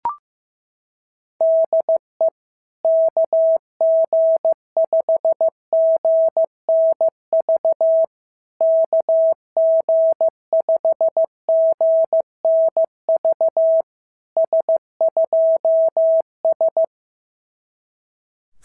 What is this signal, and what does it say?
Morse code, heard as a keyed tone, about 14 wpm, 650 Hz. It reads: DE KG5GNV KG5GNV S2S
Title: SOTA QSO Practice 15wpm